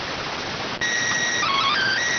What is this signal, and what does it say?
SSTV (slow-scan television), PD 290.